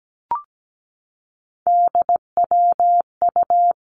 Morse code, heard as a keyed tone, about 17 wpm, 700 Hz. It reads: DWU
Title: Three Challenging Letter-Number Combinations 17wpm